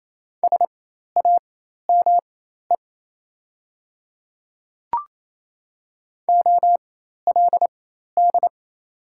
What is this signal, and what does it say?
Morse code, heard as a keyed tone, about 27 wpm, spaced as for 13 wpm, 690 Hz. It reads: SAME OLD
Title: 4x Character Spacing - Top 300 Words - 28wpm